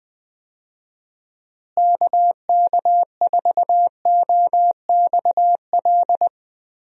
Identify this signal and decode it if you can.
Morse code, heard as a keyed tone, about 20 wpm, 700 Hz. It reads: KK4OXL
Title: USA Call Signs - 6 Characters - Repeated 2x - 20wpm